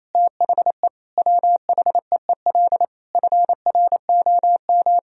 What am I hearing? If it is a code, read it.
Morse code, heard as a keyed tone, about 27 wpm, 700 Hz. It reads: THE WHEEL FROM
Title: Sentences from Top 1000 Words - Review Part 3 - 28wpm